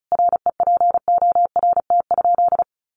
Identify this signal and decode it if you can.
Morse code, heard as a keyed tone, about 34 wpm, 700 Hz. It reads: REPORT?
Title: Sentences from Top 1200 Words 35wpm